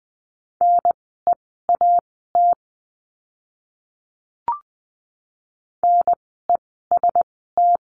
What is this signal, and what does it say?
Morse code, heard as a keyed tone, about 20 wpm, spaced as for 15 wpm, 700 Hz. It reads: NEAT NEST
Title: 036 Lesson 7f - Letter s - Words 20wpm